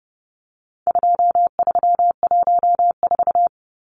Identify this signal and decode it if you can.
Morse code, heard as a keyed tone, about 29 wpm, 700 Hz. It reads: 2314
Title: Numbers - 4-Digit 30wpm